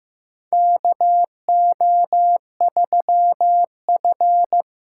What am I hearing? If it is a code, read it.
Morse code, heard as a keyed tone, about 15 wpm, 700 Hz. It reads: KO3F
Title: USA Call Signs - 4 Characters - 2x Character Spacing - 15wpm